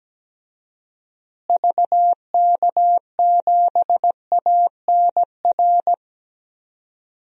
Morse code, heard as a keyed tone, about 17 wpm, 700 Hz. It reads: VK7ANR
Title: DXCC-100 Call Signs - 6 Characters - Repeated 2x - 17wpm